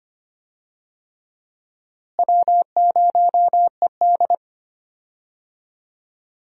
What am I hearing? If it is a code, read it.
Morse code, heard as a keyed tone, about 25 wpm, 700 Hz. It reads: W0ED
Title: USA Call Signs - 4 Characters - Repeated 3x - 25wpm